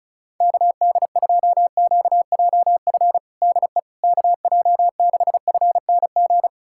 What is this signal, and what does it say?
Morse code, heard as a keyed tone, about 36 wpm, 700 Hz. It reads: KD2QJF DE KJ6FNG
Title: USA Callsign DE Callsign - 6 Characters - 35wpm